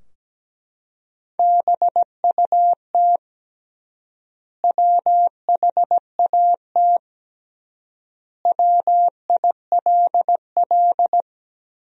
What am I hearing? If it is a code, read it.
Morse code, heard as a keyed tone, about 17 wpm, 700 Hz. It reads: BUT WHAT WILL
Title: 3x Word-Spacing - Sentences from Top 100 Words - 17wpm